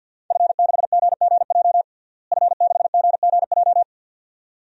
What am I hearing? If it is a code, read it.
Morse code, heard as a keyed tone, about 45 wpm, 700 Hz. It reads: F6GGJ F6GGJ
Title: DXCC-100 Call Signs - 5 Characters - Repeated 2x - 50wpm